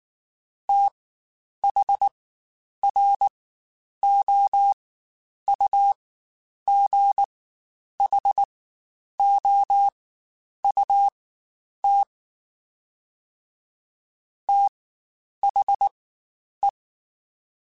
Morse code, heard as a keyed tone, about 19 wpm, spaced as for 9 wpm, 800 Hz. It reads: THROUGHOUT THE
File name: Spaced Characters / Spaced Characters-mp3s / Weather5 12 sp @18